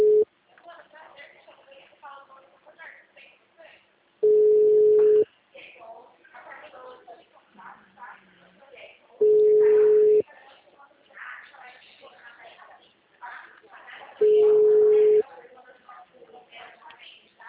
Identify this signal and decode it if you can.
Ringback tone (Europe, 425 Hz), with speech.